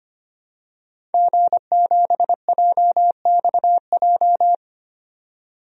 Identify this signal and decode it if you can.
Morse code, heard as a keyed tone, about 25 wpm, 700 Hz. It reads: G7JXJ
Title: DXCC-100 Call Signs - 5 Characters - 25wpm